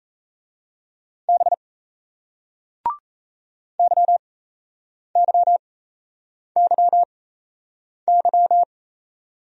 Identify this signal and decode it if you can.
Morse code, heard as a keyed tone, about 38 wpm, spaced as for 8 wpm, 700 Hz. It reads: B YYYY